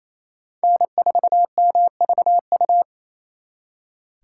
Morse code, heard as a keyed tone, about 33 wpm, 700 Hz. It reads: N4MVU
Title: Speed Racing - USA 1x3 Call Signs 28wpm